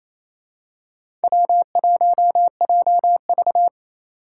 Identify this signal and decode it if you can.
Morse code, heard as a keyed tone, about 28 wpm, 700 Hz. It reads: W1JV
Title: USA Call Signs - 4 Characters - 4x Character Spacing - 28wpm